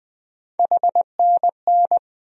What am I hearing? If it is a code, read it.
Morse code, heard as a keyed tone, about 20 wpm, 700 Hz. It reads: HNN
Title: USA Call Sign Postfixes - 3 Letters 20wpm